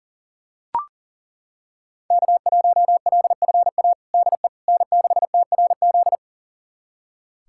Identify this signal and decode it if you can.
Morse code, heard as a keyed tone, about 37 wpm, 700 Hz. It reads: K1LFA DE N6TRZ
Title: USA Callsign DE Callsign - 5 Characters - 40wpm